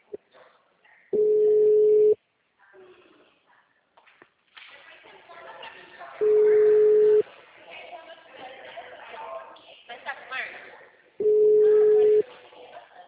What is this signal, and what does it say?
Ringback tone (Europe, 425 Hz), with speech.